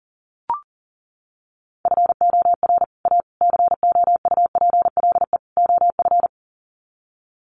Morse code, heard as a keyed tone, about 40 wpm, 700 Hz. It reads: FOR A COUPLE OF
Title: Sets of 4 Words 40wpm